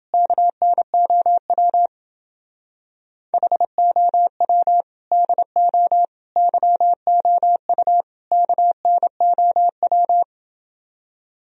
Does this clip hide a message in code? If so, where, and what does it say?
Yes: Morse code from 0.1 to 10.2 s, heard as a keyed tone, about 29 wpm, 700 Hz. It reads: KNOW HOW DO YOU KNOW